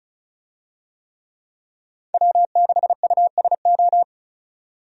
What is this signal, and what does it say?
Morse code, heard as a keyed tone, about 34 wpm, 700 Hz. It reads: W6USO
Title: USA Call Signs - 5 Characters - 35wpm